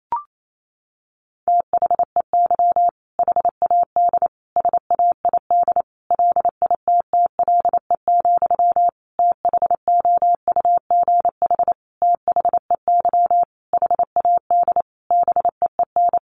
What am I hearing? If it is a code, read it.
Morse code, heard as a keyed tone, about 28 wpm, 700 Hz. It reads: THEY HAD SAID LITTLE, THOUGH THEY HAD BEEN
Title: Sentences from Top 400 Words 28wpm